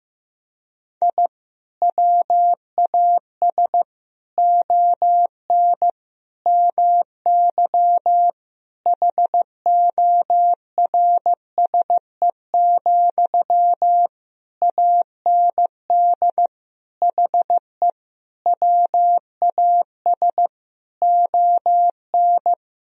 Morse code, heard as a keyed tone, about 15 wpm, 700 Hz. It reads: I WAS ON MY HORSE, AND HE WAS ON
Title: Sentences from Top 1000 Words - Review Part 3 - 15wpm